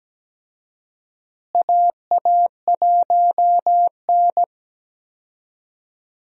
Morse code, heard as a keyed tone, about 17 wpm, 700 Hz. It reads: AA1N
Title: USA Call Signs - 4 Characters - Repeated 3x - 17wpm